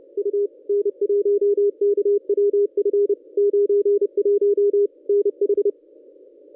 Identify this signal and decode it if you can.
Morse code, heard as a keyed tone, about 31 wpm, 440 Hz. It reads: U N1KWF 91 NH
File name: contests / ss-cw-k5tr-op-n5rz -1100